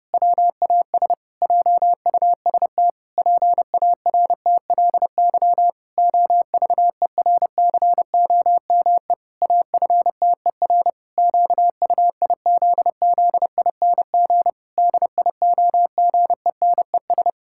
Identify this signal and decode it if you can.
Morse code, heard as a keyed tone, about 29 wpm, 700 Hz. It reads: WAS JUST PARTLY OVERCOME AFTER QUIZZING DIOGENES
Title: Pangram Challenge v2 - 30wpm